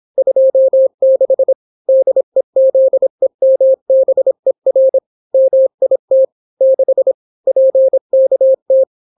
Morse code, heard as a keyed tone, about 26 wpm, 530 Hz. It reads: 26 DEZEMBER MIT 6 PKT